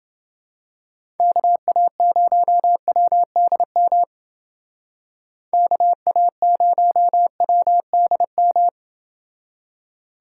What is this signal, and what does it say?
Morse code, heard as a keyed tone, about 29 wpm, 700 Hz. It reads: KA0WDM KA0WDM